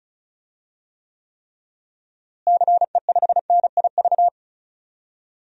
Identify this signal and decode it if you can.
Morse code, heard as a keyed tone, about 34 wpm, 700 Hz. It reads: CE5NIV
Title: DXCC-100 Call Signs - 6 Characters - Repeated 3x - 35wpm